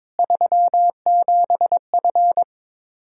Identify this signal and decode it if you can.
Morse code, heard as a keyed tone, about 23 wpm, 700 Hz. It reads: 37F
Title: Practice QSO - Exchange 2 - Info - 22wpm